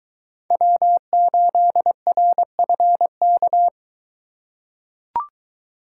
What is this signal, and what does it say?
Morse code, heard as a keyed tone, about 23 wpm, 700 Hz. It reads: W8RFK